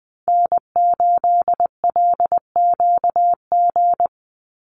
Morse code, heard as a keyed tone, about 20 wpm, 700 Hz. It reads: N8LQG
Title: USA Call Signs 20wpm